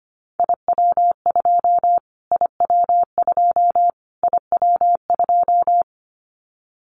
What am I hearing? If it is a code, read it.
Morse code, heard as a keyed tone, about 25 wpm, 700 Hz. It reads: IW2 IW2 IW2
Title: Callsign Prefixes for Top 25 DXCC - Repeated 3x - 25wpm